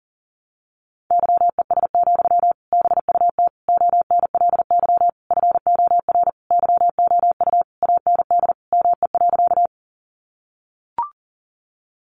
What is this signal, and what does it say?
Morse code, heard as a keyed tone, about 40 wpm, 700 Hz. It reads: YES, BUT ONLY FOR YOU AND ME.
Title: Sentences from Top 500 Words 40wpm